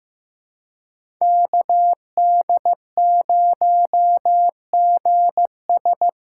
Morse code, heard as a keyed tone, about 15 wpm, 700 Hz. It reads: KD0GS
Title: USA Callsign DE Callsign - 5 Characters - 15wpm